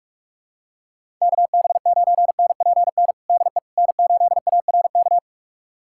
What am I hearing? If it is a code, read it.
Morse code, heard as a keyed tone, about 46 wpm, 700 Hz. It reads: KB9NPN DE N8ARK
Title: USA Callsign DE Callsign 45wpm